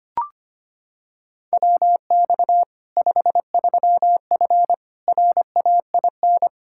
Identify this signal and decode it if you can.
Morse code, heard as a keyed tone, about 25 wpm, 700 Hz. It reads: WX 53F RAIN
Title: Practice QSO - Exchange 2 - Info - 25wpm